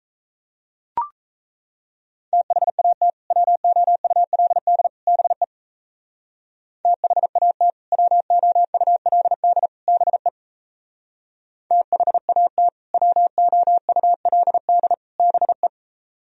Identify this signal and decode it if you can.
Morse code, heard as a keyed tone, about 35 wpm, 700 Hz. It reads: THAT WOULD BE THAT WOULD BE THAT WOULD BE